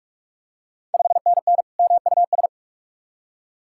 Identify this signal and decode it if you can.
Morse code, heard as a keyed tone, about 44 wpm, 690 Hz. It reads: 5NN MUS
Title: Ontario QSO Party Exchange 45wpm